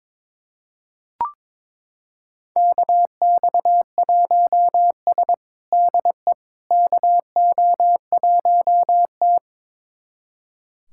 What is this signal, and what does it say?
Morse code, heard as a keyed tone, about 22 wpm, 700 Hz. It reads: KX1S DE KO1T
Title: USA Callsign DE Callsign - 4 Characters - 22wpm